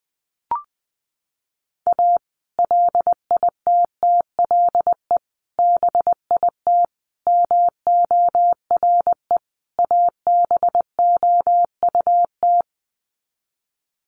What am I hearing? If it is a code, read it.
Morse code, heard as a keyed tone, about 19 wpm, 700 Hz. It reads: A LITTLE BIT MORE ABOUT
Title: Sets of 5 Words Encore2 20wpm